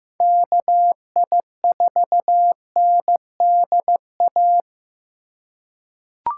Morse code, heard as a keyed tone, about 15 wpm, 700 Hz. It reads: KI4NDA
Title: USA Call Signs - 6 Characters - 4x Character Spacing - 15wpm